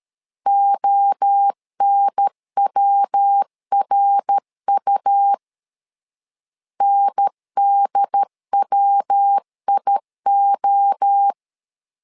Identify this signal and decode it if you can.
Morse code, heard as a keyed tone, about 13 wpm, 800 Hz. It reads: ONWRU NDWIO
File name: LEZIONI / 4 Q 12wpm 1833